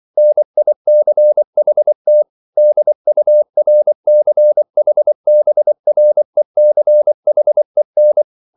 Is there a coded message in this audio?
Yes: Morse code from 0.2 to 8.2 s, heard as a keyed tone, about 24 wpm, 600 Hz. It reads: NICHT DURCHBRECHEN